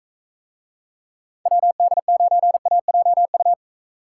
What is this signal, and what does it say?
Morse code, heard as a keyed tone, about 45 wpm, 700 Hz. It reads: WD9AJU